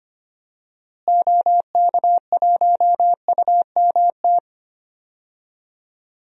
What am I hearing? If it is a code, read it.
Morse code, heard as a keyed tone, about 25 wpm, 700 Hz. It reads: OK1UMT
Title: DXCC-100 Call Signs - 6 Characters - Repeated 3x - 25wpm